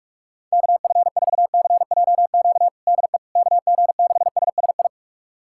Morse code, heard as a keyed tone, about 46 wpm, 700 Hz. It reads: KF4CJQ DE KG6SSI
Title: USA Callsign DE Callsign - 6 Characters - 45wpm